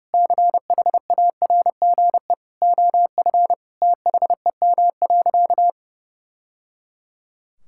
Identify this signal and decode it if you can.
Morse code, heard as a keyed tone, about 30 wpm, 700 Hz. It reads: CHARGE OF THEM.
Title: Sentences from Top 1000 Words - Review Part 6 - 30wpm